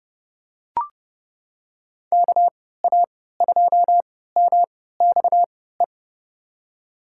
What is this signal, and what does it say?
Morse code, heard as a keyed tone, about 30 wpm, spaced as for 17 wpm, 700 Hz. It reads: KA2MXE
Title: USA Call Signs - 6 Characters - 3x Character Spacing - 30wpm